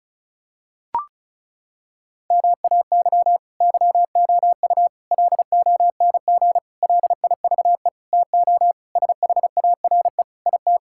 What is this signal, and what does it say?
Morse code, heard as a keyed tone, about 36 wpm, 700 Hz. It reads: MAY YOU LONG LIVE TO SHARE IT
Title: Sentences from Top 1000 Words - Review Part 6 - 35wpm